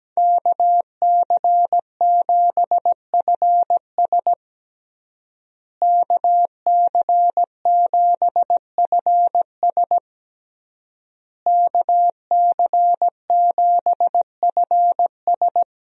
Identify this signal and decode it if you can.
Morse code, heard as a keyed tone, about 17 wpm, 700 Hz. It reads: KC7FS KC7FS KC7FS
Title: USA Call Signs - 5 Characters - Repeated 3x - 17wpm